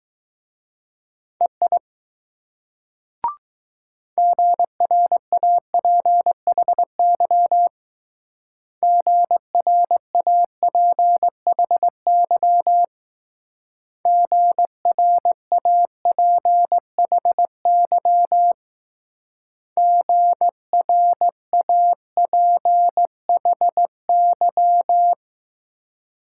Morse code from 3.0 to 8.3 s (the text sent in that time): GRAPHY